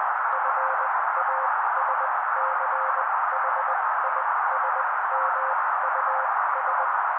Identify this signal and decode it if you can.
Morse code, heard as a keyed tone, about 20 wpm, spaced as for 16 wpm, 530 Hz. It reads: FASCHISMUS
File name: daily / 16wpm noise